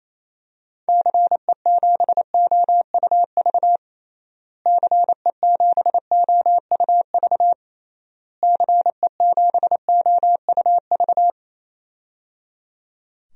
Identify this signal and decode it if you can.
Morse code, heard as a keyed tone, about 28 wpm, 700 Hz. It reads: CE7OUV CE7OUV CE7OUV
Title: DXCC-100 Call Signs - 6 Characters - Repeated 3x - 28wpm